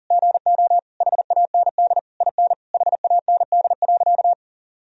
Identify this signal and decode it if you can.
Morse code, heard as a keyed tone, about 40 wpm, 700 Hz. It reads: GO HAND IN HAND.
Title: Sentences from Top 1000 Words - Review Part 3 - 40wpm